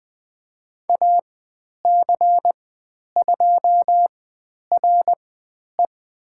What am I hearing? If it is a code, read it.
Morse code, heard as a keyed tone, about 20 wpm, spaced as for 10 wpm, 700 Hz. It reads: AC2RE
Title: 109 Lesson 18g - Letter m - Call Signs 20wpm